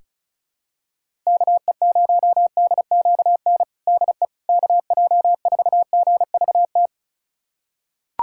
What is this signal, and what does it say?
Morse code, heard as a keyed tone, about 34 wpm, 700 Hz. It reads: KE0DQN DE KJ4GVT
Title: USA Callsign DE Callsign - 6 Characters - 35wpm